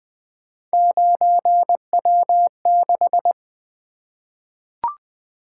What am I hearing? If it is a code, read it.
Morse code, heard as a keyed tone, about 20 wpm, 700 Hz. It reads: 9W6
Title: Three Letter-Number Combinations - 20wpm